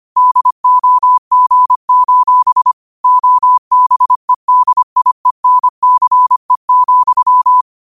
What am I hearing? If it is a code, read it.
Morse code, heard as a keyed tone, about 25 wpm, 1000 Hz. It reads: DOG: OBEDIENCE,